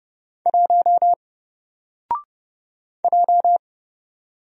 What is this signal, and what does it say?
Morse code, heard as a keyed tone, about 30 wpm, 700 Hz. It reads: 1 J
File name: Single-Letter-Number-30wpm